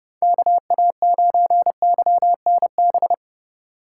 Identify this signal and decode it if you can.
Morse code, heard as a keyed tone, about 33 wpm, 700 Hz. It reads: KA9YNB